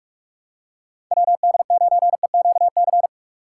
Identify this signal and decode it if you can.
Morse code, heard as a keyed tone, about 47 wpm, 700 Hz. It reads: WD9EQC
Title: USA Call Signs - 6 Characters - 4x Character Spacing - 45wpm